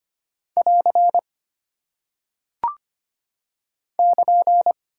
Morse code, heard as a keyed tone, about 25 wpm, 700 Hz. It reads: <AR> <KN>
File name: Single Punctuation-Prosign - 25wpm